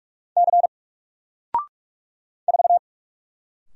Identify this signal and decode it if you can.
Morse code, heard as a keyed tone, about 44 wpm, 700 Hz. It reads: C 4
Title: Single Letter-Number Alt 45wpm